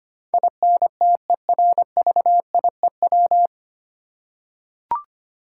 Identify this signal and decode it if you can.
Morse code, heard as a keyed tone, about 25 wpm, 700 Hz. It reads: INTERVIEW
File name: Nine Letter Words - 25wpm